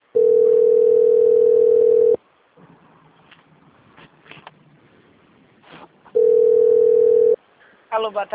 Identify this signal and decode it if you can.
Ringback tone (North America), 440 + 480 Hz, with speech.
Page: Limeira